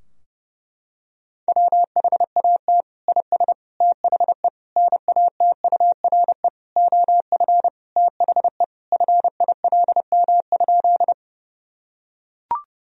Morse code, heard as a keyed tone, about 32 wpm, 700 Hz. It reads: WHAT IS THE NATURE OF THE FILM?
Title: Sentences from Top 1000 Words - Review Part 3 - 30wpm